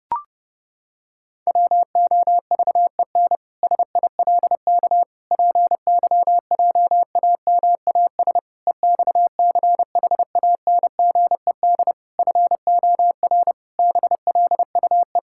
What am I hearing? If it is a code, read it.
Morse code, heard as a keyed tone, about 30 wpm, 700 Hz. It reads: WOVEN SILK PYJAMAS EXCHANGED FOR BLUE
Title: Pangram Challenge v2 - 30wpm